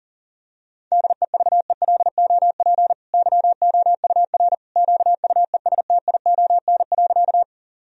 Morse code, heard as a keyed tone, about 40 wpm, 700 Hz. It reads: DEVELOP YOUR QUESTION.
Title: Sentences from Top 1000 Words - Review Part 2 - 40wpm